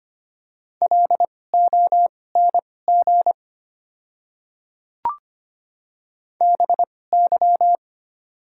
Morse code, heard as a keyed tone, about 25 wpm, spaced as for 18 wpm, 700 Hz. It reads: LONG BY
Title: 2x Character Spacing - Top 100 Words - 25wpm